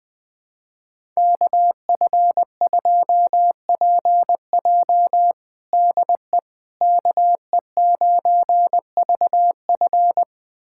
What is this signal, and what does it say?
Morse code, heard as a keyed tone, about 20 wpm, 700 Hz. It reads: KF2PJ DE KE9VF
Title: USA Callsign DE Callsign - 5 Characters - 20wpm